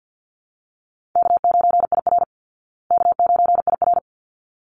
Morse code, heard as a keyed tone, about 51 wpm, 700 Hz. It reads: K9IR K9IR
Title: Easy Call Signs Repeated 2x - 50wpm